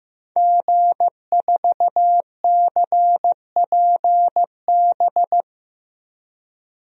Morse code, heard as a keyed tone, about 15 wpm, 700 Hz. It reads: G4CPB
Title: DXCC-100 Call Signs - 5 Characters - 15wpm